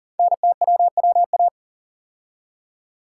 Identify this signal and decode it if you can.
Morse code, heard as a keyed tone, about 40 wpm, 700 Hz. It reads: NTWWA
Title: Code Groups - 5 Letters - 40wpm